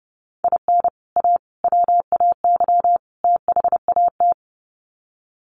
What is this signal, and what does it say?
Morse code, heard as a keyed tone, about 30 wpm, 700 Hz. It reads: IN A WAY THAT
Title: Sets of 4 Words 30wpm